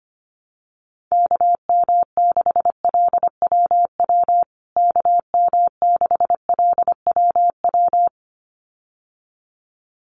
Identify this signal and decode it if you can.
Morse code, heard as a keyed tone, about 25 wpm, 700 Hz. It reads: KM6LWW KM6LWW
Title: USA Call Signs Repeated 2x 25wpm